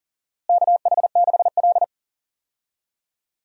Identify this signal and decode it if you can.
Morse code, heard as a keyed tone, about 40 wpm, 700 Hz. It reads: KH6L
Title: DXCC-100 Call Signs - 4 Characters - Repeated 3x - 40wpm